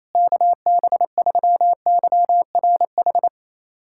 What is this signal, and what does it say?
Morse code, heard as a keyed tone, about 28 wpm, 700 Hz. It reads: KB3YRH
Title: USA Callsign DE Callsign 28wpm